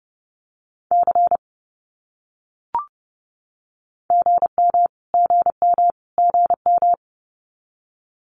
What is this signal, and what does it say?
Morse code, heard as a keyed tone, about 30 wpm, 700 Hz. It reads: C GM GM GM
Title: Abbreviations Repeated 3x - 30wpm